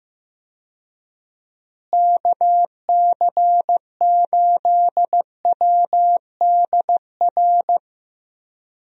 Morse code, heard as a keyed tone, about 15 wpm, 700 Hz. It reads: KC8WDR
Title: USA Call Signs - 6 Characters - Repeated 2x - 15wpm